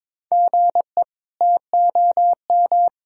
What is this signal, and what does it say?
Morse code, heard as a keyed tone, about 22 wpm, 700 Hz. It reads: GE TOM
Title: K1USN SST QSO Practice - 22wpm